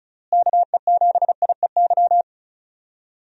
Morse code, heard as a keyed tone, about 36 wpm, 700 Hz. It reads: KE7IEY
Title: USA Call Signs - 6 Characters - Repeated 3x - 35wpm